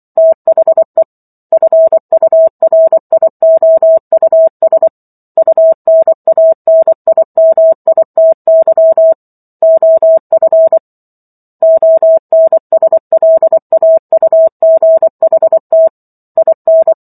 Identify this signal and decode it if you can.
Morse code, heard as a keyed tone, about 24 wpm, 650 Hz. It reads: THE FURIOUS UNANIMITY OF ONSLAUGHT IN